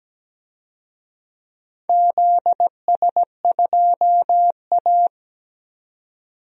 Morse code, heard as a keyed tone, about 17 wpm, 700 Hz. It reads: ZS2A
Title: DXCC-100 Call Signs - 4 Characters - Repeated 2x - 17wpm